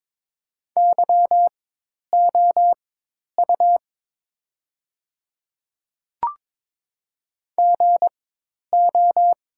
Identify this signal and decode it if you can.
Morse code, heard as a keyed tone, about 22 wpm, spaced as for 10 wpm, 700 Hz. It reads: YOU GO
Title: 4x Character Spacing - Top 100 Words - 22wpm